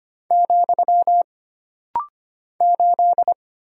Morse code, heard as a keyed tone, about 25 wpm, 700 Hz. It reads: , 8
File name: Single Character (Letter:Number:Punctuation:Prosign) - Rapid-Fire - 25wpm